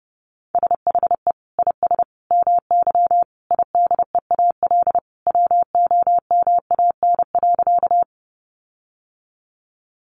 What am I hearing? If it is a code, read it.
Morse code, heard as a keyed tone, about 30 wpm, 700 Hz. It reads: SHE IS MY IDEAL WOMAN.
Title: Sentences from Top 1400 Words 30wpm